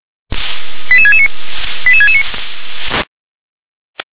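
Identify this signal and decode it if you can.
Five-tone selective call (ZVEI), heard as quick runs of five tones, tones of about 70 ms. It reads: 88669 88669
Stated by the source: ZVEI five-tone selective call received with an SDR.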